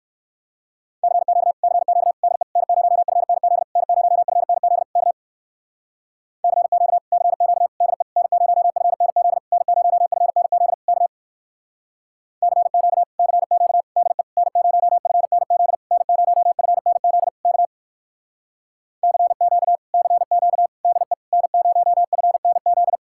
Morse code, heard as a keyed tone, about 59 wpm, 700 Hz. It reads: CQ CQ DE N0FNZ N0FNZ K CQ CQ DE N0FNZ N0FNZ K CQ CQ DE N0FNZ N0FNZ K CQ CQ DE N0FNZ